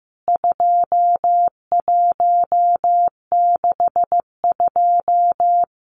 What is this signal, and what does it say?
Morse code, heard as a keyed tone, about 15 wpm, 700 Hz. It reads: 2162
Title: CWT Exchange 15wpm